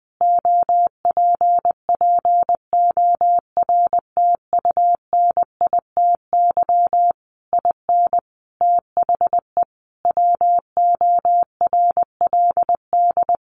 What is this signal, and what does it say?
Morse code, heard as a keyed tone, about 20 wpm, 700 Hz. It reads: OPPORTUNITY IN THE WORLD
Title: Sentences from Top 1500 Words 20wpm